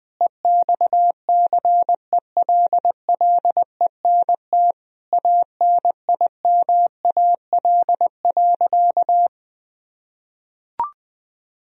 Morse code, heard as a keyed tone, about 21 wpm, 700 Hz. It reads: EXCELLENT ANIMAL.
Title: Sentences from Top 1000 Words - Review Part 1 - 20wpm